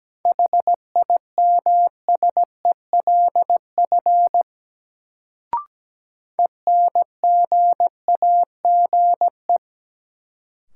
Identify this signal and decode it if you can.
Morse code, heard as a keyed tone, about 17 wpm, 700 Hz. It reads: HIMSELF ENGAGE
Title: Sentences from Top 1500 Words - New Word Drill - 17wpm